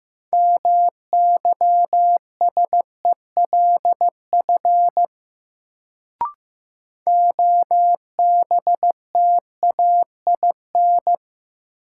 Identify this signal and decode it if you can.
Morse code, heard as a keyed tone, about 16 wpm, 700 Hz. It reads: MYSELF OBTAIN
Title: Sentences from Top 1300 Words - New Word Drill - 15wpm